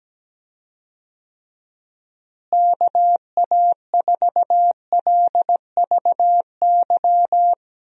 Morse code, heard as a keyed tone, about 17 wpm, 700 Hz. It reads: KA4LVY
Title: USA Call Signs - 6 Characters - Repeated 2x - 17wpm